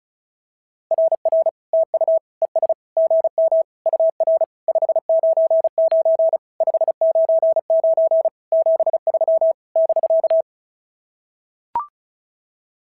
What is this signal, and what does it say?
Morse code, heard as a keyed tone, about 36 wpm, 650 Hz. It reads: RR TU ES GM UR 599 599 73 <BK>
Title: SOTA QSO Practice 35wpm